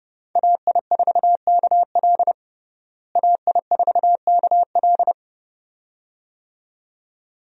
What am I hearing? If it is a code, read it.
Morse code, heard as a keyed tone, about 30 wpm, 700 Hz. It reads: AI4KL AI4KL
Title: USA Call Signs - 5 Characters - Repeated 2x - 30wpm